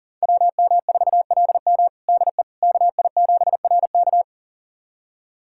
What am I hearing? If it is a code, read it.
Morse code, heard as a keyed tone, about 40 wpm, 700 Hz. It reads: WM4LM DE KI7RK
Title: USA Callsign DE Callsign - 5 Characters - 40wpm